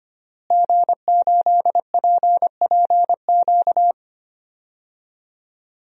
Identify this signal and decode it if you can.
Morse code, heard as a keyed tone, about 25 wpm, 700 Hz. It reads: G8PPQ
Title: DXCC-100 Call Signs - 5 Characters - Repeated 3x - 25wpm